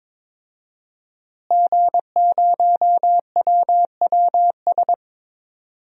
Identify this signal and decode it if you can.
Morse code, heard as a keyed tone, about 22 wpm, 700 Hz. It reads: G0WWS
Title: DXCC-100 Call Signs - 5 Characters - Repeated 2x - 22wpm